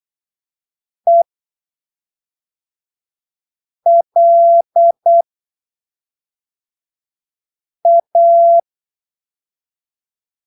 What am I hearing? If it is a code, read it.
Morse code, heard as a keyed tone, about 8 wpm, spaced as for 3 wpm, 680 Hz. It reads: ELA